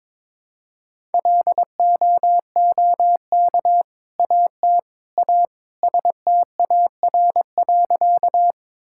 Morse code, heard as a keyed tone, about 21 wpm, 700 Hz. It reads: LOOK AT A STAR.
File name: practice / 22wpm / Sentences with 2-Word Phrases - 22wpm